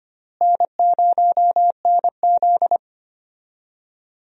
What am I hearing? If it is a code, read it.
Morse code, heard as a keyed tone, about 25 wpm, 700 Hz. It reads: N0NZ
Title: USA Call Signs - 4 Characters - Repeated 2x - 25wpm